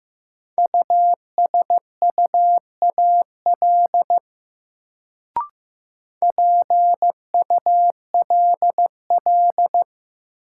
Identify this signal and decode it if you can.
Morse code, heard as a keyed tone, about 15 wpm, 700 Hz. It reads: USUAL PULL
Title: Sentences from Top 400 Words - New Word Drill - 15wpm